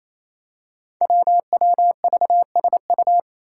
Morse code, heard as a keyed tone, about 28 wpm, 700 Hz. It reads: WWVSU
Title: Five Challenging Letter-Number Combinations 28wpm